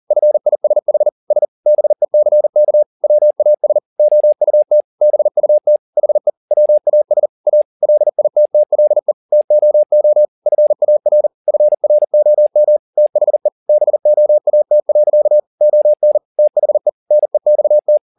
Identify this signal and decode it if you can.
Morse code, heard as a keyed tone, about 40 wpm, 600 Hz. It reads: FISHS BECK WAS OUT BUT HE WAS A LITTLE TOO FAR FROM THE BOAT. ON THE NEXT
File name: the old man and the sea 2 40wpm